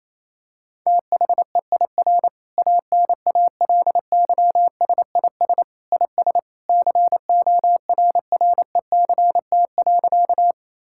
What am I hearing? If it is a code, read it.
Morse code, heard as a keyed tone, about 28 wpm, 700 Hz. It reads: THEIR ANALYSIS IS CORRECT.
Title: Sentences from Top 1000 Words - Review Part 5 - 28wpm